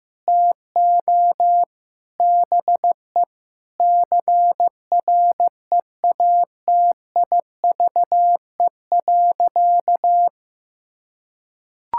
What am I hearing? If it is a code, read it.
Morse code, heard as a keyed tone, about 15 wpm, 700 Hz. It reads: TO BE CREATIVE.
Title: Sentences from Top 1000 Words - Review Part 5 - 15wpm